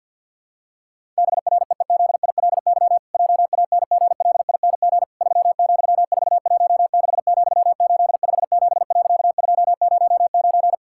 Morse code, heard as a keyed tone, about 50 wpm, 700 Hz. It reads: BREEZILY JANGLING 3,416,857.209
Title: Pangram Challenge v2 - 50wpm